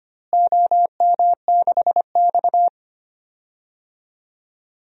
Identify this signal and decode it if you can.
Morse code, heard as a keyed tone, about 25 wpm, 700 Hz. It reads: OM6X
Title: DXCC-100 Call Signs - 4 Characters - Repeated 3x - 25wpm